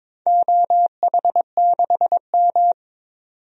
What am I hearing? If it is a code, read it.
Morse code, heard as a keyed tone, about 22 wpm, 700 Hz. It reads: OH6M
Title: DXCC-100 Call Signs - 4 Characters - 22wpm